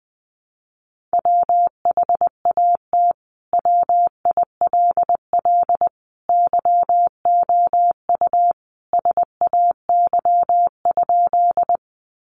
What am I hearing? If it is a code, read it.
Morse code, heard as a keyed tone, about 20 wpm, 700 Hz. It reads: WHAT WILL YOU SAY?
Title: Sentences from Top 200 Words 20wpm